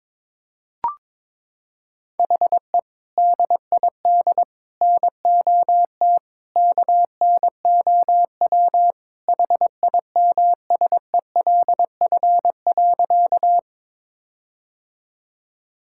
Morse code, heard as a keyed tone, about 22 wpm, 700 Hz. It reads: HE DID NOT KNOW HIMSELF.
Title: Sentences from Top 1600 Words 22wpm